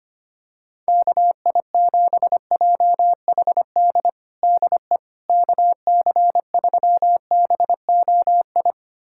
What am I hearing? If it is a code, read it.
Morse code, heard as a keyed tone, about 25 wpm, 700 Hz. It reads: KI7JHD DE KC3BOI
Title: USA Callsign DE Callsign 25wpm